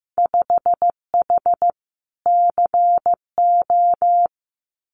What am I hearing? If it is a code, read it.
Morse code, heard as a keyed tone, about 15 wpm, 700 Hz. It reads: 5H CO
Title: Winter Field Day Exchanges Encore 15wpm